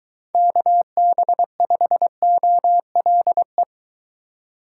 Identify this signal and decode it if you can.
Morse code, heard as a keyed tone, about 22 wpm, 700 Hz. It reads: KB5OLE